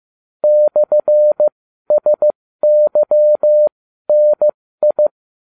Morse code, heard as a keyed tone, about 15 wpm, spaced as for 12 wpm, 600 Hz. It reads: /SYNI